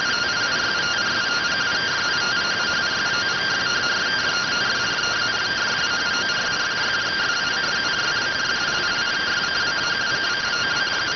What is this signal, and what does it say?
Two-tone teleprinter signal, about 170 Hz shift.